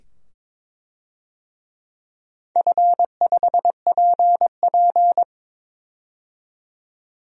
Morse code, heard as a keyed tone, about 22 wpm, 700 Hz. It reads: F5PP
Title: DXCC-100 Call Signs - 4 Characters - 22wpm